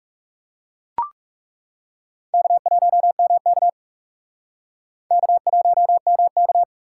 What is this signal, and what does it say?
Morse code, heard as a keyed tone, about 37 wpm, 700 Hz. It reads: K1MK K1MK
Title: Speed Racing - USA 1x2 Call Signs 30wpm